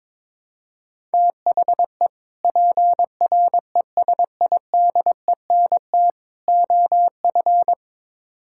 Morse code, heard as a keyed tone, about 22 wpm, 700 Hz. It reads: THE PRESIDENT OF
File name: practice / 22wpm / Build A Phrase - 5-Word Phrases - 22wpm